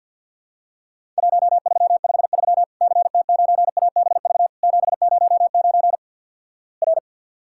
Morse code, heard as a keyed tone, about 50 wpm, 700 Hz. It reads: 1353 CT9ABV 709 R
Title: Worked All Europe DX Contest - QTC Exchange with Repeats - 50wpm